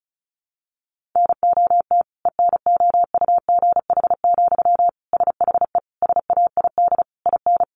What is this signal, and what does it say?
Morse code, heard as a keyed tone, about 34 wpm, 700 Hz. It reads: NOT ENOUGH, SHE SAID IN
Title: Sentences from Top 1500 Words 35wpm